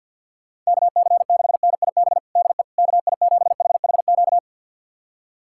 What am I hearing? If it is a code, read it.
Morse code, heard as a keyed tone, about 52 wpm, 700 Hz. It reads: KC6NIB DE KI7HHQ
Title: USA Callsign DE Callsign - 6 Characters - 50wpm